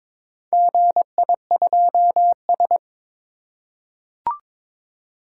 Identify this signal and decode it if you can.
Morse code, heard as a keyed tone, about 21 wpm, 700 Hz. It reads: GI2S
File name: practice / 22wpm / Four Letter-Number Combinations - 22wpm